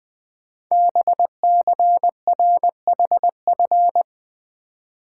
Morse code, heard as a keyed tone, about 20 wpm, 700 Hz. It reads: BCRHF
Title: Code Groups - 5 Letters - 20wpm